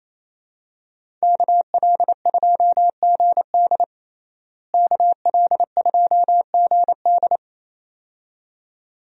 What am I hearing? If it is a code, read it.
Morse code, heard as a keyed tone, about 27 wpm, 700 Hz. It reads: KL2GD KL2GD
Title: USA Call Signs - 5 Characters - Repeated 2x - 28wpm